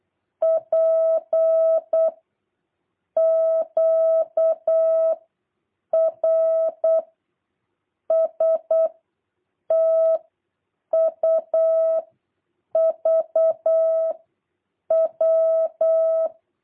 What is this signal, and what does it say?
Morse code, heard as a keyed tone, about 8 wpm, 650 Hz. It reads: P Q R S T U V W
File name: jh8cbh / 2023cw